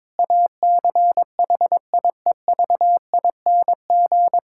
Morse code, heard as a keyed tone, about 22 wpm, 700 Hz. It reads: ACHIEVING
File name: Difficult Sentences with Hints - 22wpm